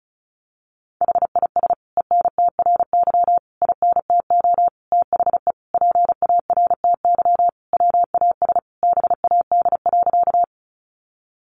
Morse code, heard as a keyed tone, about 34 wpm, 700 Hz. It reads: HIS ENTRY INTO THE PARTY WAS BAD.
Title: Sentences from Top 1500 Words 35wpm